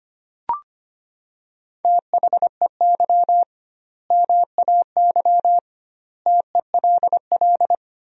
Morse code, heard as a keyed tone, about 25 wpm, 700 Hz. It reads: THEY MAY TELL
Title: 2x Word-Spacing - Sentences from Top 300 Words - 25wpm